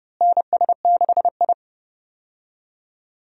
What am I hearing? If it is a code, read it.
Morse code, heard as a keyed tone, about 30 wpm, 700 Hz. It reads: NS6I
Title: USA Call Signs - 4 Characters - Repeated 2x - 30wpm